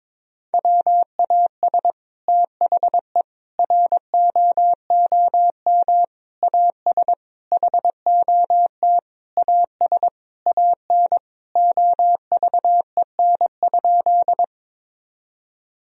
Morse code, heard as a keyed tone, about 22 wpm, 700 Hz. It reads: WAS THE ROOM AS HOT AS AN OVEN?
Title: Sentences from Top 1000 Words - Review Part 4 - 22wpm